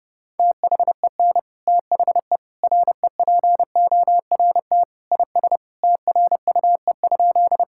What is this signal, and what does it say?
Morse code, heard as a keyed tone, about 30 wpm, 700 Hz. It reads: THEN THE REPORT IS TRUE?
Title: Sentences from Top 1000 Words - Review Part 6 - 30wpm